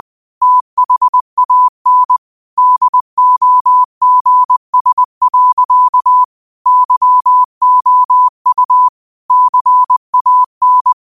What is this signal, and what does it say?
Morse code, heard as a keyed tone, about 20 wpm, 1000 Hz. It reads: THAN DOGS. YOU CAN